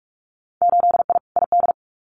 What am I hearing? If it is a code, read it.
Morse code, heard as a keyed tone, about 46 wpm, 700 Hz. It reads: 8I ID
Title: Winter Field Day Exchanges Encore 45wpm